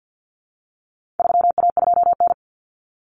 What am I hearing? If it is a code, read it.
Morse code, heard as a keyed tone, about 50 wpm, 700 Hz. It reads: 3A2N
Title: Common Cut Numbers - 4-Digit 50wpm